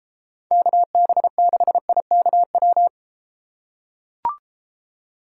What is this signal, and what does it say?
Morse code, heard as a keyed tone, about 32 wpm, 700 Hz. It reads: KB6IKW